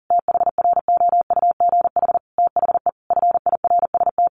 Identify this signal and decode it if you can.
Morse code, heard as a keyed tone, about 40 wpm, 700 Hz. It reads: THROUGH THE FIRST
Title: Sentences from Top 1200 Words 40wpm